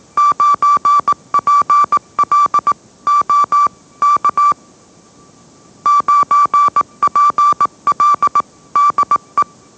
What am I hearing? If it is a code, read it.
Morse code, heard as a keyed tone, about 18 wpm, 1160 Hz. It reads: 9PL O K 9PL DE